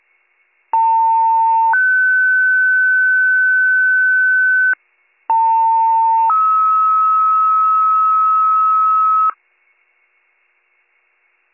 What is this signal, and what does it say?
Two-tone page, heard as a one-second tone, then a three-second tone, about 900 then 1500 Hz, about 900 then 1290 Hz. It is heard from 0.7 to 9.3 s.